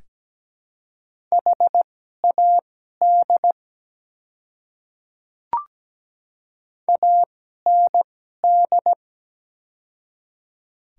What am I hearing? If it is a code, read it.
Morse code, heard as a keyed tone, about 17 wpm, spaced as for 12 wpm, 700 Hz. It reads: HAD AND
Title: 2x Character Spacing - Top 100 Words - 17wpm